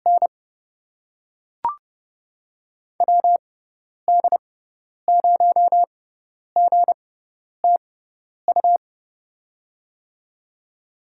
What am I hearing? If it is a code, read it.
Morse code, heard as a keyed tone, about 28 wpm, spaced as for 10 wpm, 700 Hz. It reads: N WD0GTU
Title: USA Call Signs - 6x Character Spacing - 30wpm